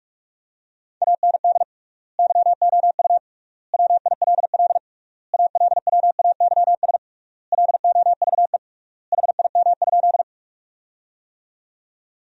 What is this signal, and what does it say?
Morse code, heard as a keyed tone, about 45 wpm, 700 Hz. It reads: AND YOU WILL ALWAYS LOVE HIM?
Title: 3x Word-Spacing - Sentences from Top 400 Words - 45wpm